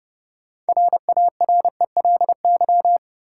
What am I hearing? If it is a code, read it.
Morse code, heard as a keyed tone, about 31 wpm, 700 Hz. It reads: RARELY
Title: Sentences from Top 1400 Words - New Word Drill - 30wpm